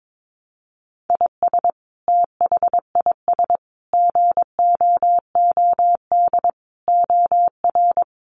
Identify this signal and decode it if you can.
Morse code, heard as a keyed tone, about 22 wpm, 700 Hz. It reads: IS THIS GOOD OR
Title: Sentences from Top 700 Words 22wpm